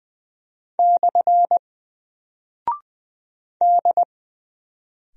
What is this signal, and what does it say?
Morse code, heard as a keyed tone, about 20 wpm, 700 Hz. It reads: / D
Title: Single Character (Letter:Number:Punctuation:Prosign) Alt 20wpm